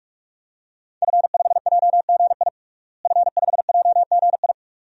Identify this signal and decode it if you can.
Morse code, heard as a keyed tone, about 41 wpm, 700 Hz. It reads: F5JGI F5JGI
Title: DXCC-100 Call Signs - 5 Characters - Repeated 2x - 45wpm